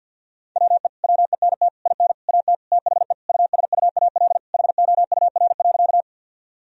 Morse code, heard as a keyed tone, about 52 wpm, 700 Hz. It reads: WE WENT IN AT THE USUAL HOUR.
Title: Sentences from Top 1000 Words - Review Part 2 - 50wpm